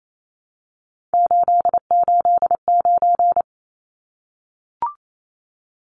Morse code, heard as a keyed tone, about 28 wpm, 700 Hz. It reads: 889
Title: Numbers - 3-Digit 28wpm